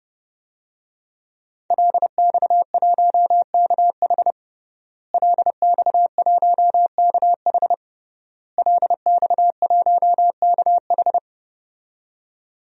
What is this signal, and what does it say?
Morse code, heard as a keyed tone, about 32 wpm, 700 Hz. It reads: LX1KH LX1KH LX1KH
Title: DXCC-100 Call Signs - 5 Characters - Repeated 3x - 30wpm